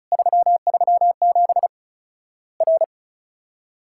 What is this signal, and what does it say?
Morse code, heard as a keyed tone, about 34 wpm, 700 Hz. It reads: 337 R
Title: Worked All Europe DX Contest - QTC Exchange - 35wpm